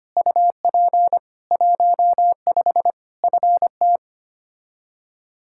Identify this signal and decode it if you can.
Morse code, heard as a keyed tone, about 25 wpm, 700 Hz. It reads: UP 15 FT
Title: Practice QSO - Exchange 2 - Info - 25wpm